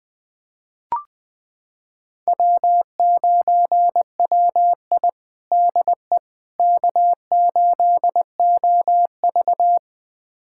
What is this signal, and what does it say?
Morse code, heard as a keyed tone, about 21 wpm, 700 Hz. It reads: W9WI DE K8OV
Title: USA Callsign DE Callsign - 4 Characters - 20wpm